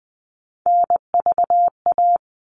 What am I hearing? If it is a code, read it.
Morse code, heard as a keyed tone, about 20 wpm, 700 Hz. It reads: NVA
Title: All Cut Numbers - 3-Digit 20wpm